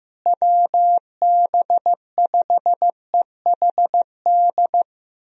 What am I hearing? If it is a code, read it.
Morse code, heard as a keyed tone, about 15 wpm, 700 Hz. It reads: WB5EHD
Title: USA Call Signs - 6 Characters - Repeated 2x - 15wpm